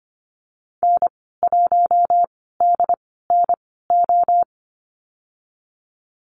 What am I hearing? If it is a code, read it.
Morse code, heard as a keyed tone, about 25 wpm, spaced as for 16 wpm, 700 Hz. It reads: N1DNO
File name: Lesson 15g - Number 5 - Call Signs 25wpm